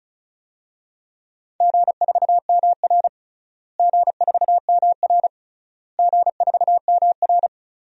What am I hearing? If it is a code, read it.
Morse code, heard as a keyed tone, about 36 wpm, 700 Hz. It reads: G4MR G4MR G4MR
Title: DXCC-100 Call Signs - 4 Characters - Repeated 3x - 35wpm